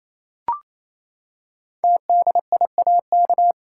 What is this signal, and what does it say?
Morse code, heard as a keyed tone, about 29 wpm, 700 Hz. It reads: TDIAK
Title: Code Groups - 5 Letters - 28wpm